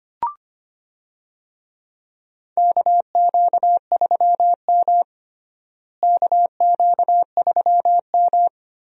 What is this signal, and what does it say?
Morse code, heard as a keyed tone, about 25 wpm, 700 Hz. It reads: KQ3M KQ3M
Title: USA Call Signs - 4 Characters - Repeated 3x - 25wpm